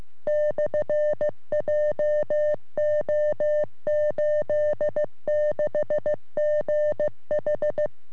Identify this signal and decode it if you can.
Morse code, heard as a keyed tone, about 15 wpm, 600 Hz. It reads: /JO86GH
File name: sm7lcb / ra3le 23cm